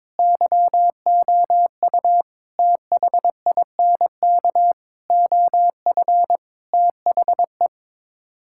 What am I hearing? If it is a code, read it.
Morse code, heard as a keyed tone, about 22 wpm, 700 Hz. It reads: YOU THINK OF THE
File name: practice / 22wpm / Sets of 6 Words - 22wpm